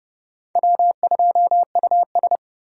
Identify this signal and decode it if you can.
Morse code, heard as a keyed tone, about 37 wpm, 700 Hz. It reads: W2US
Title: Speed Racing - USA 1x2 Call Signs 30wpm